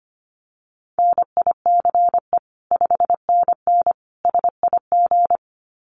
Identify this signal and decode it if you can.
Morse code, heard as a keyed tone, about 25 wpm, 700 Hz. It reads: NICE 5NN SIG
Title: Common QSO Phrases 25wpm